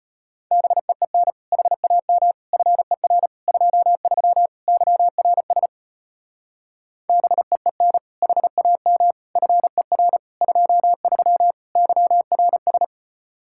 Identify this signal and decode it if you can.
Morse code, heard as a keyed tone, about 33 wpm, 700 Hz. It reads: BEEN HAM FER 23 YRS BEEN HAM FER 23 YRS